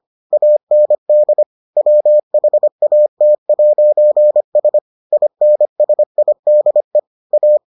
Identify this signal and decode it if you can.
Morse code, heard as a keyed tone, about 25 wpm, 600 Hz. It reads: AND WHAT'S INSIDE A